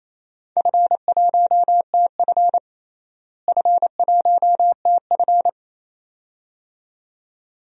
Morse code, heard as a keyed tone, about 28 wpm, 700 Hz. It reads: F1TF F1TF
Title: DXCC-100 Call Signs - 4 Characters - Repeated 2x - 28wpm